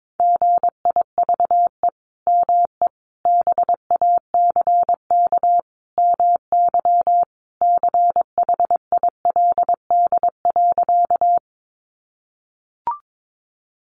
Morse code, heard as a keyed tone, about 22 wpm, 700 Hz. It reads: GIVE ME BACK MY CHILD.
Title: Sentences from Top 800 Words 22wpm